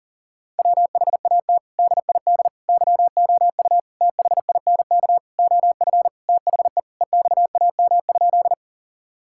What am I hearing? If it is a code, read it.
Morse code, heard as a keyed tone, about 40 wpm, 700 Hz. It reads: WHAT DID YOU THINK OF THE EXAM?
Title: Sentences from Top 1000 Words - Review Part 5 - 40wpm